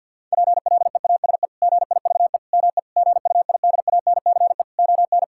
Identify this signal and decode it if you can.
Morse code, heard as a keyed tone, about 50 wpm, 700 Hz. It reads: PLEASE GIVE ME GUIDANCE ON
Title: Sentences from Top 1600 Words 50wpm